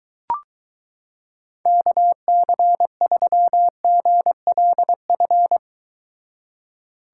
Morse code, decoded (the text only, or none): KC3GLF